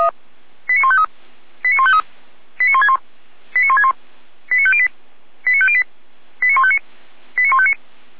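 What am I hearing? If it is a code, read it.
Five-tone selective call (ZVEI), heard as quick runs of five tones, tones of about 70 ms. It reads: 89153 89153 89161 89161 89508 89508 89159 89159
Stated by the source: ZVEI five-tone selective call received on SDR.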